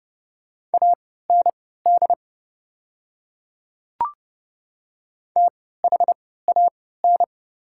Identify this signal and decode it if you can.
Morse code, heard as a keyed tone, about 30 wpm, spaced as for 17 wpm, 690 Hz. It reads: AND THAN
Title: 3x Character Spacing - Top 200 Words - 30wpm